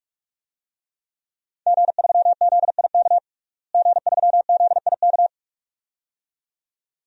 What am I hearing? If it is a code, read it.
Morse code, heard as a keyed tone, about 46 wpm, 700 Hz. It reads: G3ZIK G3ZIK
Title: DXCC-100 Call Signs - 5 Characters - Repeated 2x - 45wpm